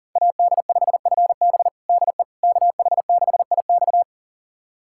Morse code, heard as a keyed tone, about 38 wpm, 700 Hz. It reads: AD5FB DE KH6IX
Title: USA Callsign DE Callsign - 5 Characters - 40wpm